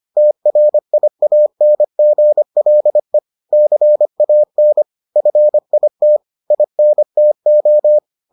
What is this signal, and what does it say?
Morse code, heard as a keyed tone, about 25 wpm, 600 Hz. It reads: TRIANGLE CAN FIT INTO